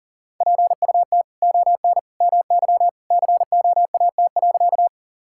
Morse code, heard as a keyed tone, about 40 wpm, 700 Hz. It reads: PUT ON MY COAT.
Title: Sentences from Top 1600 Words 40wpm